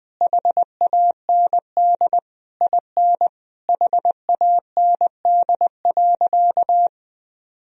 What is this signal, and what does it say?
Morse code, heard as a keyed tone, about 20 wpm, 700 Hz. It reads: HAND IN HAND.
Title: Sentences from Top 1000 Words - Review Part 4 - 20wpm